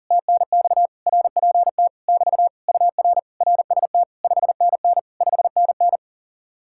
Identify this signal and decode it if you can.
Morse code, heard as a keyed tone, about 48 wpm, 700 Hz. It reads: TNX RPT <BT> UR RST 5NN 5NN
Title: Speed-Racing Common QSO Phrases - 40wpm